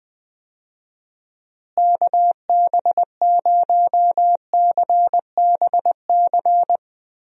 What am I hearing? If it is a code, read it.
Morse code, heard as a keyed tone, about 20 wpm, 700 Hz. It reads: KB0CBC
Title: USA Call Signs - 6 Characters - Repeated 2x - 20wpm